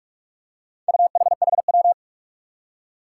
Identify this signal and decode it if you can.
Morse code, heard as a keyed tone, about 44 wpm, 700 Hz. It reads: UHHW
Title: Four Challenging Letter-Number Combinations 45wpm